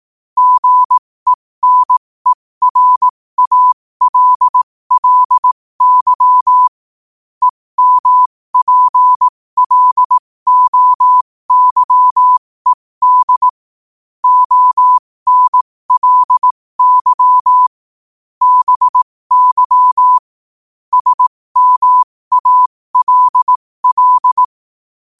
Morse code, heard as a keyed tone, about 18 wpm, spaced as for 15 wpm, 1000 Hz. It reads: GENERALLY EMPLOYED ONLY BY SMALL
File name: qotdfiles / 15wpm